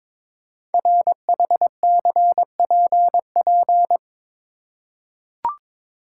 Morse code, heard as a keyed tone, about 22 wpm, 700 Hz. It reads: RHCPP
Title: Code Groups - 5 Letters - 22wpm